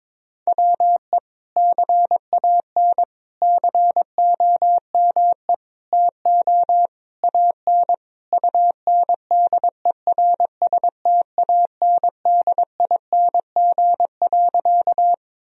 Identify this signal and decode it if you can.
Morse code, heard as a keyed tone, about 21 wpm, 700 Hz. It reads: WE CAN COME TO AN UNDERSTANDING.
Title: Sentences from Top 1000 Words - Review Part 3 - 22wpm